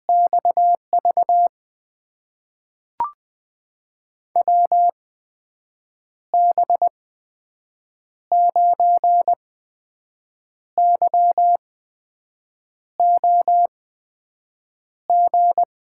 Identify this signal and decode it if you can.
Morse code, heard as a keyed tone, about 20 wpm, spaced as for 6 wpm, 700 Hz. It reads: XV WB9YOG
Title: USA Call Signs - 6 Characters - 8x Character Spacing - 20wpm